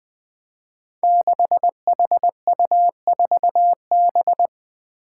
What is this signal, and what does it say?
Morse code, heard as a keyed tone, about 20 wpm, 700 Hz. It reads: 6HU4B
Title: Five Challenging Letter-Number Combinations 20wpm